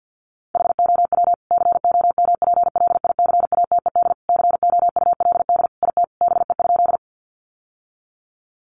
Morse code, heard as a keyed tone, about 50 wpm, 700 Hz. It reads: HOW COMPLICATED COULD IT BE?
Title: Sentences from Top 1200 Words 50wpm